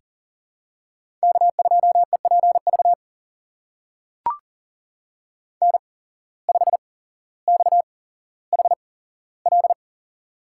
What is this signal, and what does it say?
Morse code, heard as a keyed tone, about 40 wpm, spaced as for 17 wpm, 700 Hz. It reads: K2EPV N5XHL
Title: USA Call Signs - 8x Character Spacing - 40wpm